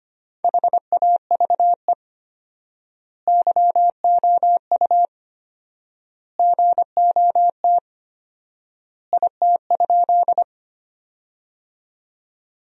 Morse code, heard as a keyed tone, about 25 wpm, 700 Hz. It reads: HAVE YOU GOT IT?
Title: 4x Word-Spacing - Sentences from Top 300 Words - 25wpm